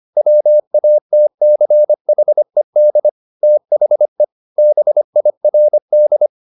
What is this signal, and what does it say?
Morse code, heard as a keyed tone, about 25 wpm, 600 Hz. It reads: WATCHED THE BIRD
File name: the old man and the sea part 1 25wpm